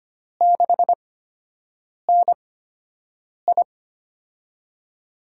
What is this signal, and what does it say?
Morse code, heard as a keyed tone, about 25 wpm, spaced as for 7 wpm, 700 Hz. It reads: 6NI
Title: USA Call Signs - 8x Character Spacing - 25wpm